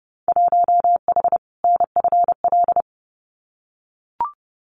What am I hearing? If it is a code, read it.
Morse code, heard as a keyed tone, about 30 wpm, 700 Hz. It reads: 1H NFL
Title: Winter Field Day Exchanges 30wpm